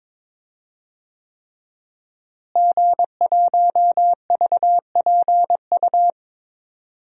Morse code, heard as a keyed tone, about 21 wpm, 700 Hz. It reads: G1VPU
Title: DXCC-100 Call Signs - 5 Characters - Repeated 3x - 22wpm